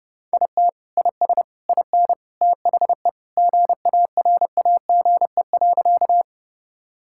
Morse code, heard as a keyed tone, about 32 wpm, 700 Hz. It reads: IT IS IN THE GARAGE.
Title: Sentences from Top 1600 Words 30wpm